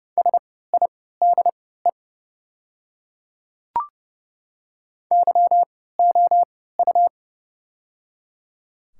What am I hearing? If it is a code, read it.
Morse code, heard as a keyed tone, about 30 wpm, spaced as for 17 wpm, 700 Hz. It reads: SIDE YOU
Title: 3x Character Spacing - Top 100 Words - 30wpm